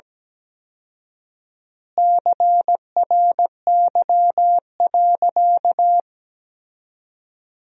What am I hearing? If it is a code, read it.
Morse code, heard as a keyed tone, about 17 wpm, 700 Hz. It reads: CRY.
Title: 4x Word-Spacing - Sentences from Top 500 Words - 17wpm